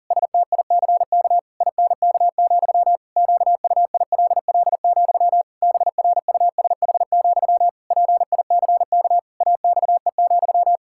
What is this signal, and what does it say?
Morse code, heard as a keyed tone, about 37 wpm, 700 Hz. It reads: STICK INK, QUILL, BRUSH, PICK AXE,
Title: Pangram Challenge v2 - 40wpm